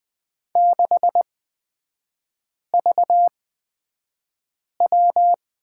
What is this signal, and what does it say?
Morse code, heard as a keyed tone, about 20 wpm, spaced as for 5 wpm, 690 Hz. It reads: 6VW
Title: 255 Lesson 41e - Period - Three Characters 20wpm